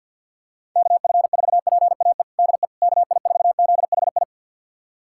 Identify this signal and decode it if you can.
Morse code, heard as a keyed tone, about 52 wpm, 700 Hz. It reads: KF4PAE DE KI4ZHI
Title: USA Callsign DE Callsign - 6 Characters - 50wpm